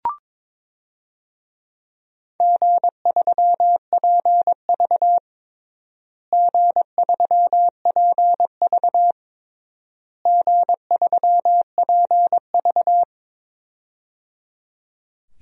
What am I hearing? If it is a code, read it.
Morse code, heard as a keyed tone, about 22 wpm, 700 Hz. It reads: G3PV G3PV G3PV
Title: DXCC-100 Call Signs - 4 Characters - Repeated 3x - 22wpm